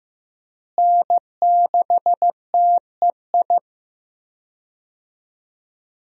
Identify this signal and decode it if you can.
Morse code, heard as a keyed tone, about 15 wpm, 700 Hz. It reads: N6TEI
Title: USA Call Signs - 5 Characters - Repeated 2x - 15wpm